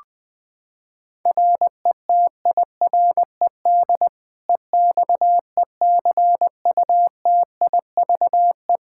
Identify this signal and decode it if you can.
Morse code, heard as a keyed tone, about 20 wpm, 700 Hz. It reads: RETIRED EXECUTIVE
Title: K1USN SST QSO Practice 20wpm